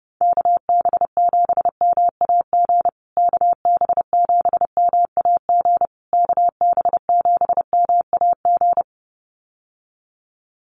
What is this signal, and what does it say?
Morse code, heard as a keyed tone, about 30 wpm, 700 Hz. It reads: KB7MAG KB7MAG KB7MAG
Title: USA Call Signs Repeated 3x 30wpm